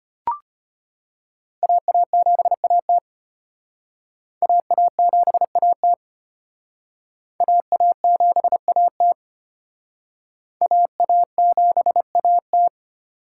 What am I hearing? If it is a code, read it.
Morse code, heard as a keyed tone, about 33 wpm, 700 Hz. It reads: AA7AT AA7AT AA7AT AA7AT